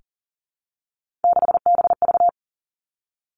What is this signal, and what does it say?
Morse code, heard as a keyed tone, about 42 wpm, 700 Hz. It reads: 6BV
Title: Three Challenging Letter-Number Combinations 40wpm